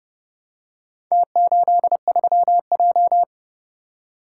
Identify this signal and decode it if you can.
Morse code, heard as a keyed tone, about 29 wpm, 700 Hz. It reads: T83J